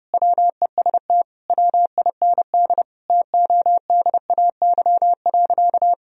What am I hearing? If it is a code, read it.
Morse code, heard as a keyed tone, about 32 wpm, 700 Hz. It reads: WEST WIND TODAY.
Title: Sentences from Top 1000 Words - Review Part 2 - 30wpm